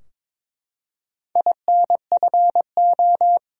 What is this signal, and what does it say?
Morse code, heard as a keyed tone, about 22 wpm, 700 Hz. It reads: INFO
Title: Top 180 QSO Elements 22wpm